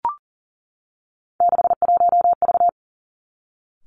Morse code, heard as a keyed tone, about 40 wpm, 700 Hz. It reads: 61V
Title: Three Challenging Letter-Number Combinations 40wpm